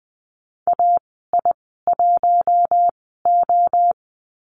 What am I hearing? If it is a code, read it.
Morse code, heard as a keyed tone, about 20 wpm, spaced as for 14 wpm, 700 Hz. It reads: AI1O
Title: 062 Lesson 11g - Letter h - Call Signs 20wpm